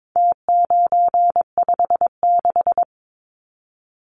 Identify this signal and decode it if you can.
Morse code, heard as a keyed tone, about 22 wpm, 700 Hz. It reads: T956
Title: Numbers - Contest Style - 4-Digit 22wpm